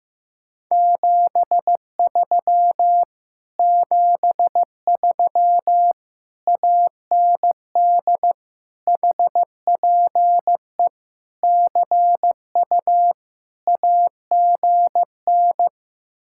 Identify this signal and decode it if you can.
Morse code, heard as a keyed tone, about 15 wpm, 700 Hz. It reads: 73 73 AND HPE CU AGN
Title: Speed-Racing Common QSO Phrases Encore - 15wpm